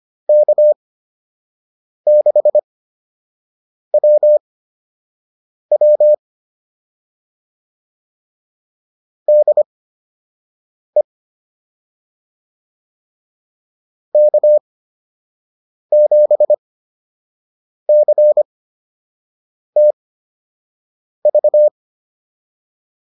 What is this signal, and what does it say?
Morse code, heard as a keyed tone, about 25 wpm, spaced as for 6 wpm, 600 Hz. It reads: K6WW DE K7CTV